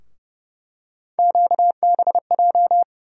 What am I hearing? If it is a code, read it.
Morse code, heard as a keyed tone, about 30 wpm, 700 Hz. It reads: QBJ
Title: USA Call Sign Postfixes - 3 Letters 30wpm